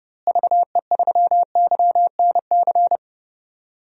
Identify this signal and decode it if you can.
Morse code, heard as a keyed tone, about 30 wpm, 700 Hz. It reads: VE3YNC
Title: DXCC-100 Call Signs - 6 Characters - Repeated 2x - 30wpm